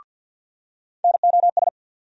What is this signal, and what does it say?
Morse code, heard as a keyed tone, about 47 wpm, 700 Hz. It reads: NOS
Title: USA Call Sign Postfixes - 3 Letters 50wpm